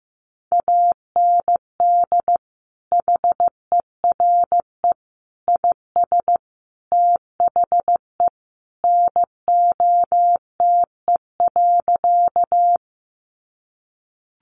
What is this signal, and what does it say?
Morse code, heard as a keyed tone, about 15 wpm, 700 Hz. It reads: AND HERE IS THE NOTE.
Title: Sentences from Top 500 Words 15wpm